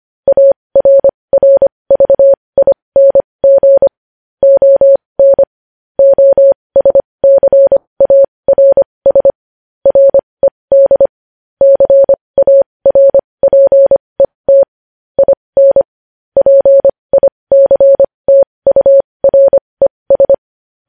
Morse code, heard as a keyed tone, about 25 wpm, spaced as for 20 wpm, 550 Hz. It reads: ARRVING ON OSCARS RED CARPET IN PICTURES
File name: podcast / News headlines-1x-25x20